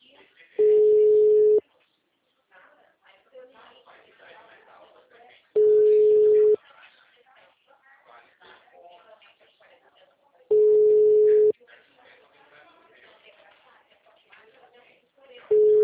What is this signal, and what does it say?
Ringback tone (Europe, 425 Hz), with speech.
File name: out- -2020- - - .2433